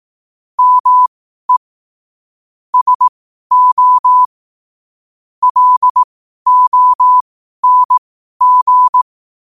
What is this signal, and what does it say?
Morse code, heard as a keyed tone, about 18 wpm, spaced as for 12 wpm, 1000 Hz. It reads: ME SO LONG
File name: qotdfiles / 12wpm